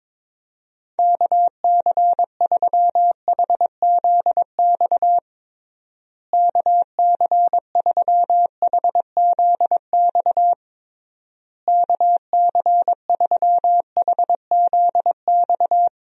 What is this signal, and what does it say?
Morse code, heard as a keyed tone, about 22 wpm, 700 Hz. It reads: KC3HZX KC3HZX KC3HZX
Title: USA Call Signs - 6 Characters - Repeated 3x - 22wpm